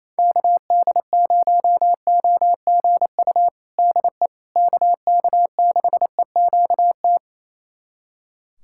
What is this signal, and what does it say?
Morse code, heard as a keyed tone, about 28 wpm, 700 Hz. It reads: KD0OGU DE KK6EQT
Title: USA Callsign DE Callsign - 6 Characters - 28wpm